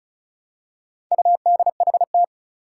Morse code, heard as a keyed tone, about 36 wpm, 700 Hz. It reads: UDHT
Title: Four Challenging Letter-Number Combinations 35wpm